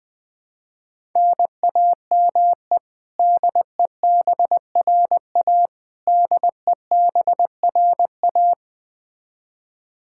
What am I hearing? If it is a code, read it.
Morse code, heard as a keyed tone, about 20 wpm, 700 Hz. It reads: NAME DEBRA DEBRA
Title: Practice QSO - Exchange 1 - Beginning - 20wpm